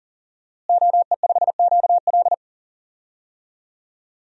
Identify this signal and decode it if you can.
Morse code, heard as a keyed tone, about 38 wpm, 690 Hz. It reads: OE5QL
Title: DXCC-100 Call Signs - 5 Characters - Repeated 2x - 40wpm